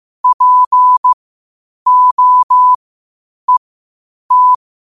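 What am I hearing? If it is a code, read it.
Morse code, heard as a keyed tone, about 15 wpm, spaced as for 8 wpm, 1000 Hz. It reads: POET
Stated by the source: Morse practice at 8 WPM from Morse Resource.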